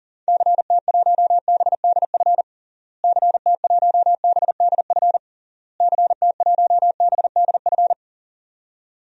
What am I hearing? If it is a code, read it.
Morse code, heard as a keyed tone, about 40 wpm, 700 Hz. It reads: CT1BDF CT1BDF CT1BDF
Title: DXCC-100 Call Signs - 6 Characters - Repeated 3x - 40wpm